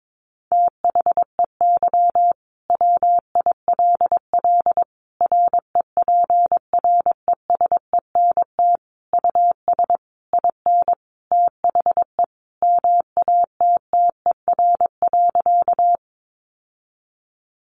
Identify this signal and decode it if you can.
Morse code, heard as a keyed tone, about 22 wpm, 700 Hz. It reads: THEY WILL REPRESENT US IN THE MATTER.
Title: Sentences from Top 1200 Words 22wpm